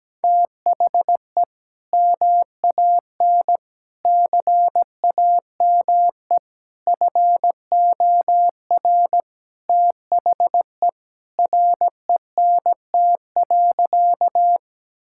Morse code, heard as a keyed tone, about 17 wpm, 700 Hz. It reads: THE MAN CAME FOR THE RENT.
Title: Sentences from Top 1000 Words - Review Part 5 - 17wpm